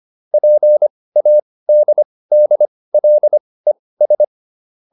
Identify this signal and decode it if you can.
Morse code, heard as a keyed tone, about 25 wpm, spaced as for 18 wpm, 600 Hz. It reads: PADDLES